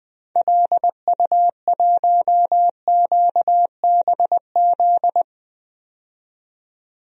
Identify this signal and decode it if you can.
Morse code, heard as a keyed tone, about 21 wpm, 700 Hz. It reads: LU1QBZ
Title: DXCC-100 Call Signs - 6 Characters - 20wpm